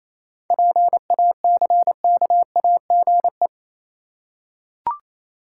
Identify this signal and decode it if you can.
Morse code, heard as a keyed tone, about 28 wpm, 700 Hz. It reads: PACKAGE
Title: Seven Letter Words - 28wpm